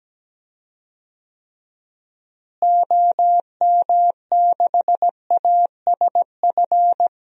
Morse code, heard as a keyed tone, about 17 wpm, 700 Hz. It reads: OM6ASF
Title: DXCC-100 Call Signs - 6 Characters - Repeated 3x - 17wpm